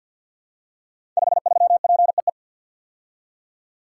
Morse code, heard as a keyed tone, about 45 wpm, 700 Hz. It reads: 53PEE
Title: Code Groups - 5 Letter:Numbers - 50wpm